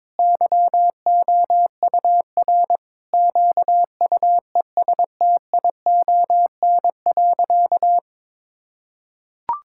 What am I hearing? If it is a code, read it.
Morse code, heard as a keyed tone, about 22 wpm, 700 Hz. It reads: YOUR QUESTION.
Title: Sentences from Top 1000 Words - Review Part 6 - 22wpm